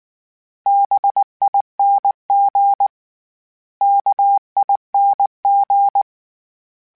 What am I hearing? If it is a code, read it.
Morse code, heard as a keyed tone, about 19 wpm, 800 Hz. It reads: BING KING
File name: Special INGs / Special INGs-mp3s / Sngl Ing @18